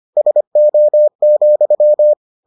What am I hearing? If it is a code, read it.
Morse code, heard as a keyed tone, about 25 wpm, 600 Hz. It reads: SO,